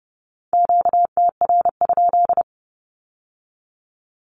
Morse code, heard as a keyed tone, about 30 wpm, 700 Hz. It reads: QTR?
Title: Q-Codes with Spoken Meaning - 30wpm